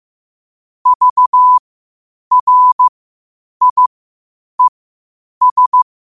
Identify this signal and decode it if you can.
Morse code, heard as a keyed tone, about 15 wpm, spaced as for 8 wpm, 1000 Hz. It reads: VRIES
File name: qotdfiles / 8wpm